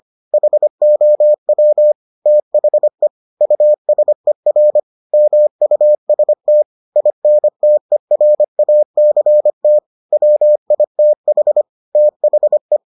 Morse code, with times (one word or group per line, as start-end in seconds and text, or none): 0.3-1.9 HOW
2.3-3.1 THE
3.4-4.8 USER
5.1-6.6 MUST
7.0-9.8 INTERACT
10.1-11.6 WITH
12.0-12.8 THE